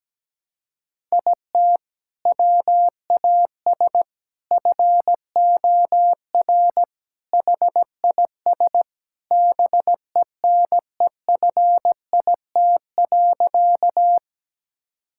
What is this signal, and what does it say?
Morse code, heard as a keyed tone, about 17 wpm, 700 Hz. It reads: IT WAS FOR HIS BENEFIT.
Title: Sentences from Top 1000 Words - Review Part 6 - 17wpm